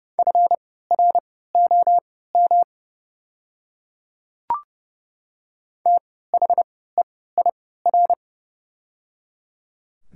Morse code, heard as a keyed tone, about 30 wpm, spaced as for 17 wpm, 700 Hz. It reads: FROM THEIR
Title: 3x Character Spacing - Top 100 Words - 30wpm